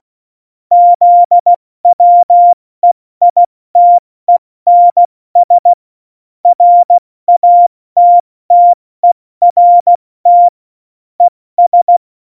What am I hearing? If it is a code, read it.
Morse code, heard as a keyed tone, about 16 wpm, 700 Hz. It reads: ZWEITENS RATTERT ES
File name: Morsekurs / Witze / Witz 26. Januar 2021, 70 F80